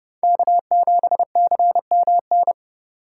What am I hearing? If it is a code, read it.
Morse code, heard as a keyed tone, about 40 wpm, 700 Hz. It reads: K7CMN